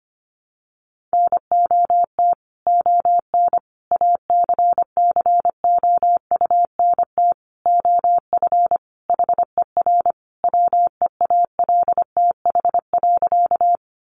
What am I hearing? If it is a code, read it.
Morse code, heard as a keyed tone, about 25 wpm, 700 Hz. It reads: NOT ON ACCOUNT OF HER WEALTH.
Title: Sentences from Top 1200 Words 25wpm